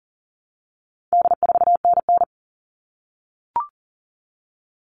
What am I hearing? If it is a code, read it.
Morse code, heard as a keyed tone, about 40 wpm, 700 Hz. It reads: D4NN
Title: All Cut Numbers - 4-Digit 40wpm